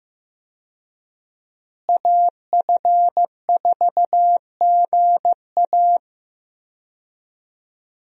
Morse code, heard as a keyed tone, about 15 wpm, 700 Hz. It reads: AF4GA
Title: USA Call Signs - 5 Characters - Repeated 2x - 15wpm